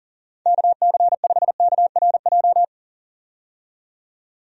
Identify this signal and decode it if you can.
Morse code, heard as a keyed tone, about 40 wpm, 700 Hz. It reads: KC5KRJ
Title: USA Call Signs - 6 Characters - 40wpm